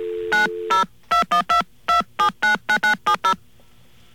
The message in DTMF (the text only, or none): 902122799970